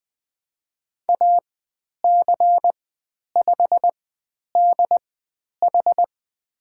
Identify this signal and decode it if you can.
Morse code, heard as a keyed tone, about 20 wpm, spaced as for 10 wpm, 700 Hz. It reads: AC5DH
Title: 095 Lesson 16g - Letter c - Call Signs 20wpm